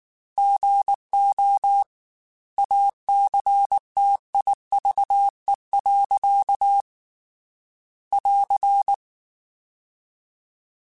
Morse code, heard as a keyed tone, about 19 wpm, 800 Hz. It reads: GO ACTIVE. <AR>
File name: Technical Subjects / Technical Subjects-mp3s / OR Gate @18